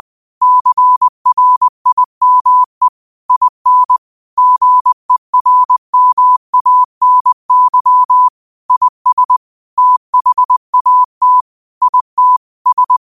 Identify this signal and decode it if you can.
Morse code, heard as a keyed tone, about 20 wpm, 1000 Hz. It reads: CRIME IN GERMANY IS THAT IT S
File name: qotdfiles / 20wpm